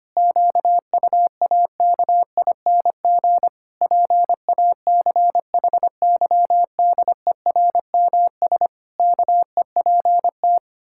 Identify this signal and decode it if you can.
Morse code, heard as a keyed tone, about 25 wpm, 700 Hz. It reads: QUAKING PACHYDERMS KEPT
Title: Pangram Challenge v2 - 25wpm